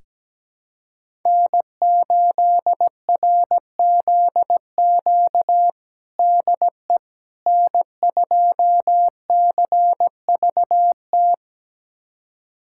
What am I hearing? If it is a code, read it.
Morse code, heard as a keyed tone, about 17 wpm, 700 Hz. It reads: N8RZQ DE N2CVT
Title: USA Callsign DE Callsign - 5 Characters - 17wpm